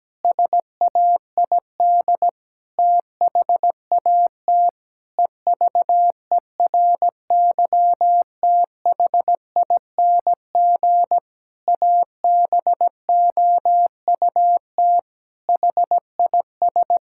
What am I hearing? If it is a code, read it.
Morse code, heard as a keyed tone, about 17 wpm, 700 Hz. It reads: SAID THAT EVERYTHING ABOUT HIS
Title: Pangram Challenge v2 - 17wpm